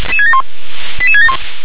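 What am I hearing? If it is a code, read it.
Five-tone selective call (ZVEI), heard as quick runs of five tones, tones of about 70 ms. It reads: 88671
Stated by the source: ZVEI five-tone selective call received on an SDR.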